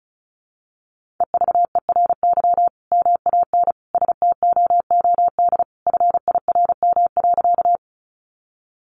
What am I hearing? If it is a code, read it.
Morse code, heard as a keyed tone, about 34 wpm, 700 Hz. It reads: EVERY MAN STOOD FIRM.
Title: Sentences from Top 1400 Words 35wpm